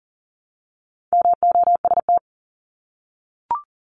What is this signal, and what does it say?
Morse code, heard as a keyed tone, about 40 wpm, 700 Hz. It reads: MOST
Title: LY Suffix - Top 100 Words with Base Words 40wpm